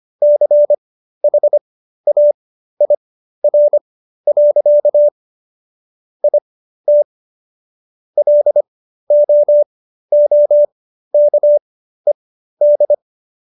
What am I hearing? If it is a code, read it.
Morse code, heard as a keyed tone, about 25 wpm, spaced as for 13 wpm, 590 Hz. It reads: CHAIR. IT LOOKED